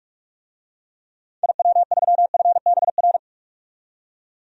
Morse code, heard as a keyed tone, about 41 wpm, 700 Hz. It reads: IW3FBR
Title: DXCC-100 Call Signs - 6 Characters - 45wpm